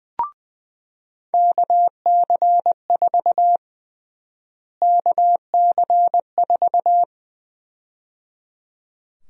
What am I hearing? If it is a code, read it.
Morse code, heard as a keyed tone, about 21 wpm, 700 Hz. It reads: KC4 KC4
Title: USA Call Sign Prefixes - Repeated 2x - 20wpm